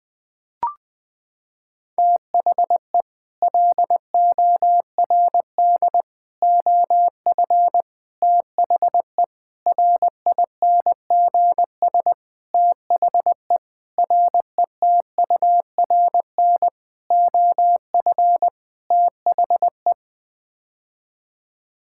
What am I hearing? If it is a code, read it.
Morse code, heard as a keyed tone, about 20 wpm, 700 Hz. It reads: THE LORD OF THE RINGS THE RETURN OF THE
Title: Sets of 9 Words - 20wpm